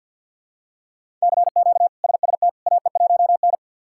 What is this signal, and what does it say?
Morse code, heard as a keyed tone, about 50 wpm, 700 Hz. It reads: CQ SST AE1N
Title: K1USN SST QSO Practice - 50wpm